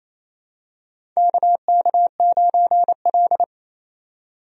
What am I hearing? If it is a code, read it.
Morse code, heard as a keyed tone, about 27 wpm, 700 Hz. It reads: KK9L
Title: USA Call Signs - 4 Characters - 28wpm